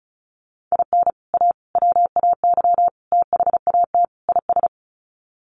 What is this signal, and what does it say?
Morse code, heard as a keyed tone, about 34 wpm, 700 Hz. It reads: IN A WAY THAT IS
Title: Sets of 5 Words Encore 35wpm